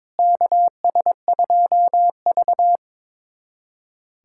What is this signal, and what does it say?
Morse code, heard as a keyed tone, about 21 wpm, 700 Hz. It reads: KS2V
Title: USA Callsign DE Callsign - 4 Characters - 22wpm